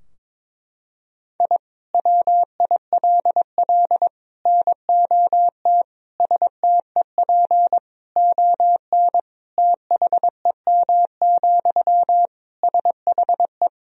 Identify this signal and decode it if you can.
Morse code, heard as a keyed tone, about 22 wpm, 700 Hz. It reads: I WILL NOT STEP ON THEM, SHE
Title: Sentences from Top 1000 Words - Review Part 2 - 22wpm